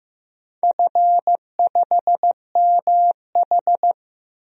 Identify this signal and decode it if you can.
Morse code, heard as a keyed tone, about 15 wpm, 700 Hz. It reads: F5MH
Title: DXCC-100 Call Signs - 4 Characters - Repeated 3x - 15wpm